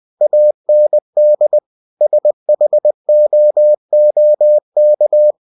Morse code, heard as a keyed tone, about 20 wpm, 600 Hz. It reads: AND SHOOK